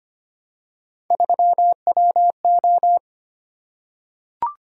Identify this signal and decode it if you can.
Morse code, heard as a keyed tone, about 25 wpm, 700 Hz. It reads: 3WO
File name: Three Letter-Number Combinations - 25wpm